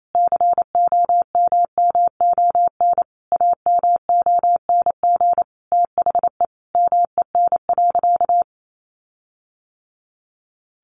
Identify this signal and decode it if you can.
Morse code, heard as a keyed tone, about 28 wpm, 700 Hz. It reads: COMMON AMONG THE MEN.
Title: Sentences from Top 1400 Words 28wpm